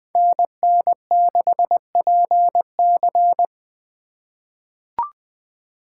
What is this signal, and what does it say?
Morse code, heard as a keyed tone, about 20 wpm, 700 Hz. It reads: NN6PC
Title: USA Call Signs - 5 Characters - 2x Character Spacing - 20wpm